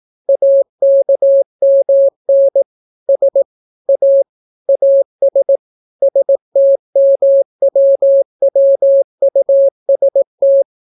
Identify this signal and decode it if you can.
Morse code, heard as a keyed tone, about 18 wpm, 550 Hz. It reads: AKMN S A AS STMWWUST